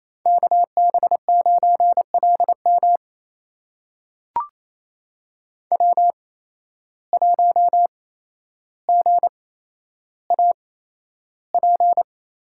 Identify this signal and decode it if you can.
Morse code, heard as a keyed tone, about 28 wpm, spaced as for 12 wpm, 700 Hz. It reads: KB9LM W1GAP
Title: USA Call Signs - 5 Characters - 8x Character Spacing - 28wpm